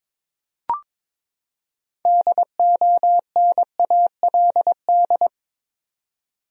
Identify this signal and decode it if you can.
Morse code, heard as a keyed tone, about 22 wpm, 700 Hz. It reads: DONALD
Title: USA Common Names - 22wpm